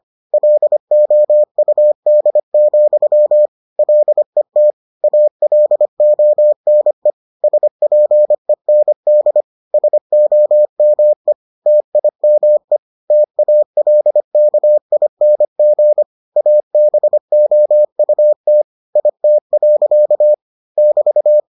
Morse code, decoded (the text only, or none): LOUD, LET ALONE SPEND SOME TIME TALKING ABOUT IT. <BT>